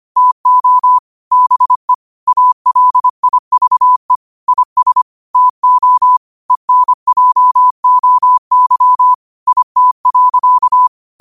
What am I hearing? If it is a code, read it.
Morse code, heard as a keyed tone, about 25 wpm, 1000 Hz. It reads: TO BE ALIVE IS TO ENJOY IT.